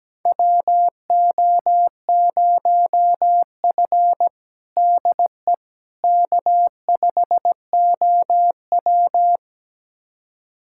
Morse code, heard as a keyed tone, about 17 wpm, 700 Hz. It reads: WO0F DE K5OW
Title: USA Callsign DE Callsign - 4 Characters - 17wpm